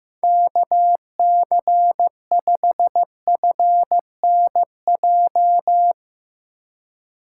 Morse code, heard as a keyed tone, about 16 wpm, 700 Hz. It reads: KC5FNJ
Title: USA Call Signs - 6 Characters - 4x Character Spacing - 15wpm